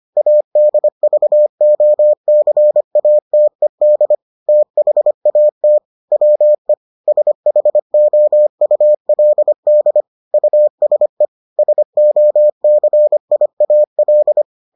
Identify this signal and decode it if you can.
Morse code, heard as a keyed tone, about 25 wpm, 600 Hz. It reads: ADVOCATED THAT WE SHOULD USE SOCIAL